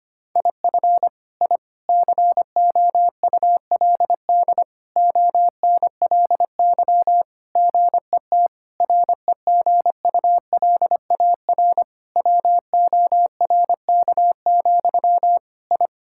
Morse code, heard as a keyed tone, about 25 wpm, 700 Hz. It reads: IF I COULD ONLY GET REGULAR WORK, I
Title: Sentences from Top 1000 Words - Review Part 6 - 25wpm